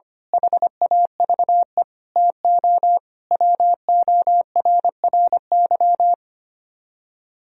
Morse code, heard as a keyed tone, about 25 wpm, 700 Hz. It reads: HAVE TO WORRY
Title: Speed-Racing - Sets of 5 Words 25wpm